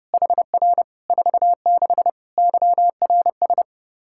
Morse code, heard as a keyed tone, about 30 wpm, 700 Hz. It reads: HR 46 YRS
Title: K1USN SST QSO Practice 30wpm